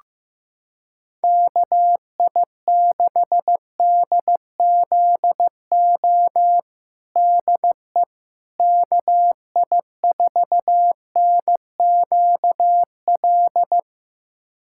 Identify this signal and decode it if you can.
Morse code, heard as a keyed tone, about 15 wpm, 700 Hz. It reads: KI6DZO DE KI4NQL
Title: USA Callsign DE Callsign 15wpm